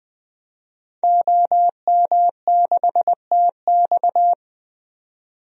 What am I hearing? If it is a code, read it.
Morse code, heard as a keyed tone, about 20 wpm, 700 Hz. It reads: OM6TX
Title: DXCC-100 Call Signs - 5 Characters - 20wpm